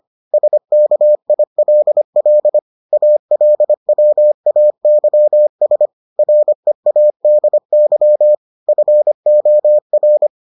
Morse code, heard as a keyed tone, about 25 wpm, 600 Hz. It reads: SKILL ALWAYS READY FOR